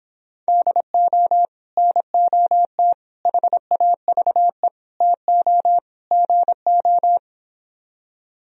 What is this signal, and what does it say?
Morse code, heard as a keyed tone, about 26 wpm, 700 Hz. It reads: DO NOT HAVE TO GO